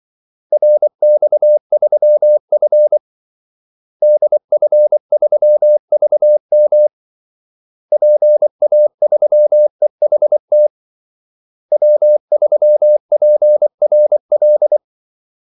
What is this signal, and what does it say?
Morse code, heard as a keyed tone, about 24 wpm, 600 Hz. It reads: RX3F DF3VM PA3EHT W3PRL
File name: Callsigns-24wpm